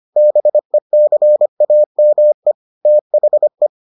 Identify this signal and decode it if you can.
Morse code, heard as a keyed tone, about 25 wpm, 600 Hz. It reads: BECAME THE